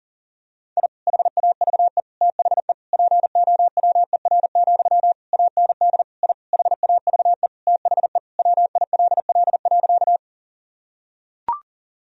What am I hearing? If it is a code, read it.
Morse code, heard as a keyed tone, about 37 wpm, 700 Hz. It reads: I HAVE THE POWER, AND I HAVE THE WILL.
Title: Sentences from Top 1000 Words - Review Part 2 - 40wpm